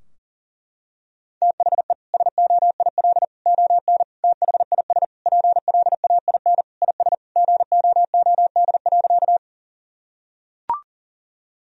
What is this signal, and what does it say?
Morse code, heard as a keyed tone, about 40 wpm, 700 Hz. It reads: THE SOIL ON THIS PLAIN IS GOOD.
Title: Sentences from Top 1000 Words - Review Part 4 - 40wpm